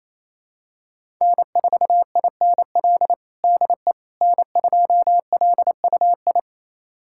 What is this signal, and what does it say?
Morse code, heard as a keyed tone, about 28 wpm, 700 Hz. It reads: N4INL DE N2LUI
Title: USA Callsign DE Callsign - 5 Characters - 28wpm